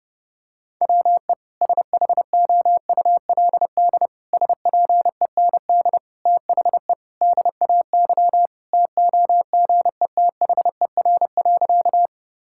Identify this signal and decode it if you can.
Morse code, heard as a keyed tone, about 32 wpm, 700 Hz. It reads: WE SHOULD SPEND THE DAY TOGETHER.
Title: Sentences from Top 1000 Words - Review Part 5 - 30wpm